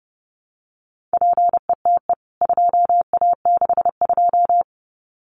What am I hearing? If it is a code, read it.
Morse code, heard as a keyed tone, about 30 wpm, 700 Hz. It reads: PETE 2A62
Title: CWT Exchange 30wpm